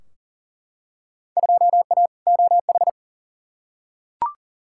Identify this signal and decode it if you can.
Morse code, heard as a keyed tone, about 40 wpm, 700 Hz. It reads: 2A OH
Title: Field Day Exchanges - 40wpm